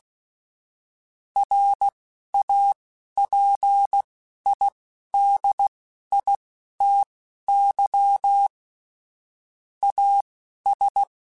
Morse code, heard as a keyed tone, about 16 wpm, spaced as for 11 wpm, 800 Hz. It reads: RAPIDITY AS
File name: Technical Subjects / Technical Subjects-mp3s / XL @10